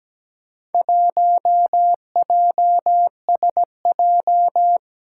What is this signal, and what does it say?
Morse code, heard as a keyed tone, about 17 wpm, 700 Hz. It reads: 1JSJ
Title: Four Challenging Letter-Number Combinations 17wpm